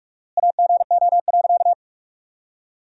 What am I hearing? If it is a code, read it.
Morse code, heard as a keyed tone, about 44 wpm, 690 Hz. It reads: AGO.
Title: 4x Word-Spacing - Sentences from Top 500 Words - 45wpm